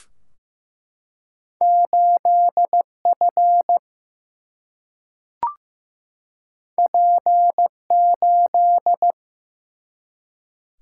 Morse code, heard as a keyed tone, about 15 wpm, 700 Hz. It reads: 8 F P 8
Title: Single-Letter Single-Number 15wpm